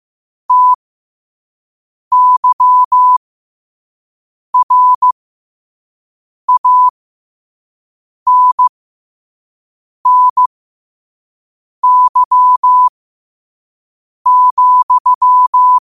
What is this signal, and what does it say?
Morse code, heard as a keyed tone, about 15 wpm, spaced as for 5 wpm, 1000 Hz. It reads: TYRANNY,
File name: qotdfiles / 5wpm